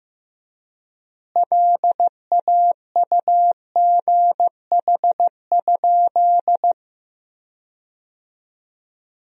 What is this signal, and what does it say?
Morse code, heard as a keyed tone, about 15 wpm, 700 Hz. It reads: LAUGH?
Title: 3x Word-Spacing - Sentences from Top 500 Words - 15wpm